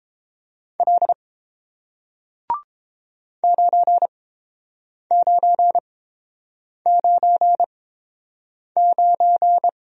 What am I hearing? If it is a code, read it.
Morse code, heard as a keyed tone, about 30 wpm, spaced as for 7 wpm, 700 Hz. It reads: L 9999